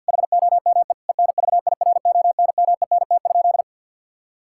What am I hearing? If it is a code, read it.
Morse code, heard as a keyed tone, about 50 wpm, 700 Hz. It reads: HOME ENVIRONMENT?
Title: Sentences from Top 1000 Words - Review Part 5 - 50wpm